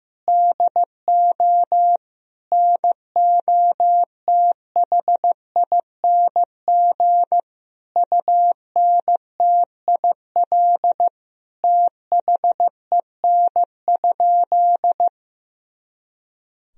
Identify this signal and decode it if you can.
Morse code, heard as a keyed tone, about 16 wpm, 700 Hz. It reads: DO NOTHING UNTIL THEN?
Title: Sentences from Top 500 Words and Abbreviations 15wpm